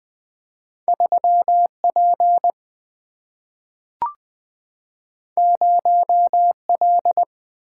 Morse code, heard as a keyed tone, about 20 wpm, 700 Hz. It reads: 3P 0L
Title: Single-Letter Single-Number 20wpm